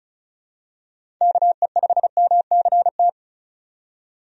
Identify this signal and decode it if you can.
Morse code, heard as a keyed tone, about 36 wpm, 700 Hz. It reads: KE5MCT
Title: USA Call Signs - 6 Characters - 35wpm